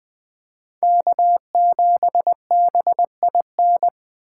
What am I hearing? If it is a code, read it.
Morse code, heard as a keyed tone, about 21 wpm, 700 Hz. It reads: K7BIN